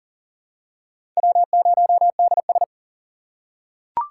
Morse code, heard as a keyed tone, about 39 wpm, 700 Hz. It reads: W0DS
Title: USA Call Signs - 6x Character Spacing - 40wpm